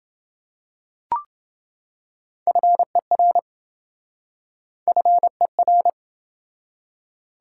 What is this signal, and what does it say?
Morse code, heard as a keyed tone, about 27 wpm, 700 Hz. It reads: FER FER